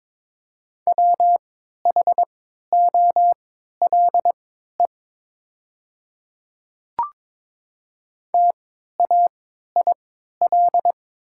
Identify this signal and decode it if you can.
Morse code, heard as a keyed tone, about 22 wpm, spaced as for 12 wpm, 700 Hz. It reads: WHOLE TAIL
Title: 3x Character Spacing - Top 500 Words - 22wpm